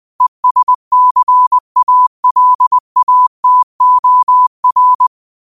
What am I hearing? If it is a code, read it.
Morse code, heard as a keyed tone, about 20 wpm, 1000 Hz. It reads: ESCALATOR
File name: qotdfiles / 20wpm